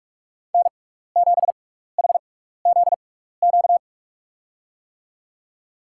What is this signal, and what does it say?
Morse code, heard as a keyed tone, about 44 wpm, spaced as for 15 wpm, 690 Hz. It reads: N7HZQ
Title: USA Call Signs - 5 Characters - 6x Character Spacing - 45wpm